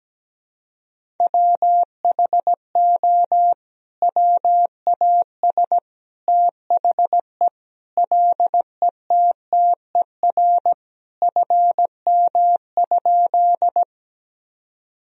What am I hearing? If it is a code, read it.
Morse code, heard as a keyed tone, about 17 wpm, 700 Hz. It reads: WHO WAS THE LETTER FM?
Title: Sentences from Top 300 Words and Abbreviations 17wpm